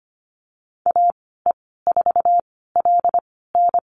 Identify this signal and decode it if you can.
Morse code, heard as a keyed tone, about 25 wpm, spaced as for 16 wpm, 700 Hz. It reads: AE4LN
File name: Lesson 15g - Number 5 - Call Signs 25wpm